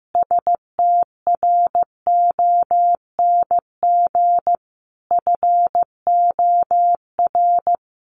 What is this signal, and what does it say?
Morse code, heard as a keyed tone, about 15 wpm, 700 Hz. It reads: STRONG FOR
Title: Sentences from Top 600 Words 15wpm